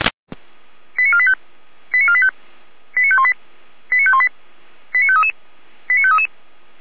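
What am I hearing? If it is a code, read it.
Five-tone selective call (ZVEI), heard as quick runs of five tones, tones of about 70 ms. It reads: 89485 89485 89518 89518 89533 89533
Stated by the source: ZVEI five-tone call 89022 received on SDR, Feuerwehr Orlamünde.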